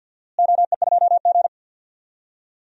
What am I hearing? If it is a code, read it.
Morse code, heard as a keyed tone, about 49 wpm, 700 Hz. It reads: OE2G
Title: DXCC-100 Call Signs - 4 Characters - Repeated 3x - 50wpm